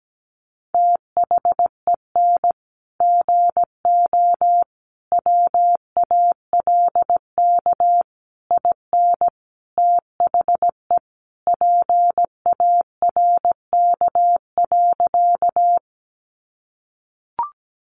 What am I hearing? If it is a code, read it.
Morse code, heard as a keyed tone, about 17 wpm, 700 Hz. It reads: THEN GO WALK IN THE PARK.
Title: Sentences from Top 1300 Words 17wpm